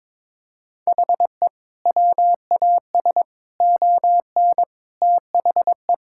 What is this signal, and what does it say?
Morse code, heard as a keyed tone, about 22 wpm, 700 Hz. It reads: HE WAS ON THE
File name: practice / 22wpm / Sentences with 4-Word Phrases - 22wpm